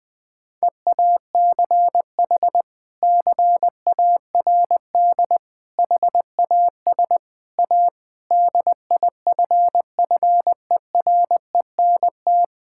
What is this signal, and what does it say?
Morse code, heard as a keyed tone, about 20 wpm, 700 Hz. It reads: EACH CARD HAS A DIFFERENT
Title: Sentences from Top 1000 Words - Review Part 6 - 20wpm